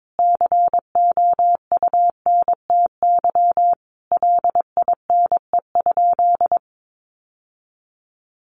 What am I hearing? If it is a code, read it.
Morse code, heard as a keyed tone, about 22 wpm, 700 Hz. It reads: COUNTY LINE?
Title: Sentences from Top 1200 Words 22wpm